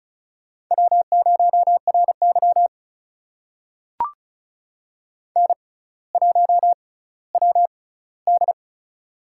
Morse code, heard as a keyed tone, about 36 wpm, spaced as for 18 wpm, 700 Hz. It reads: W0RY N1WD
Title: USA Call Signs - 4 Characters - 6x Character Spacing - 35wpm